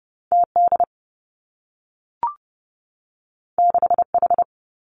Morse code, heard as a keyed tone, about 30 wpm, 700 Hz. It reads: T D 6 H
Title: Two Challenging Letter-Number Combinations 30wpm